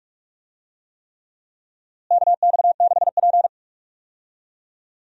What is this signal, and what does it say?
Morse code, heard as a keyed tone, about 42 wpm, 700 Hz. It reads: KX6P
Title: USA Call Signs - 4 Characters - Repeated 3x - 45wpm